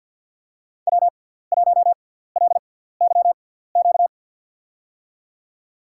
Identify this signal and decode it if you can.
Morse code, heard as a keyed tone, about 50 wpm, spaced as for 17 wpm, 690 Hz. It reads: W1LYQ
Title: USA Call Signs - 5 Characters - 6x Character Spacing - 50wpm